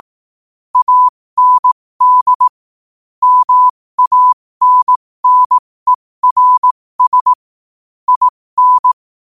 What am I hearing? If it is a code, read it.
Morse code, heard as a keyed tone, about 18 wpm, spaced as for 15 wpm, 1000 Hz. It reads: AND MANNERS IN
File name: qotdfiles / 15wpm